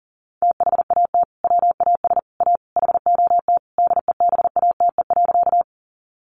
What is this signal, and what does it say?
Morse code, heard as a keyed tone, about 40 wpm, 700 Hz. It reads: THAT WAS A HOT DEBATE.
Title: Sentences from Top 1100 Words 40wpm